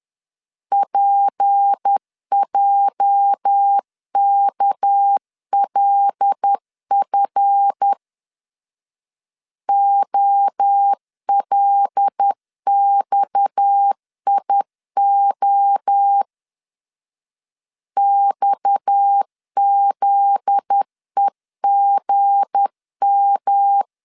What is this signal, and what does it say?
Morse code, heard as a keyed tone, about 10 wpm, 800 Hz. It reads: PJKLF OLXIO XZEGM
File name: LEZIONI / 8 YJ45 / 8 YJ 10wpm 2651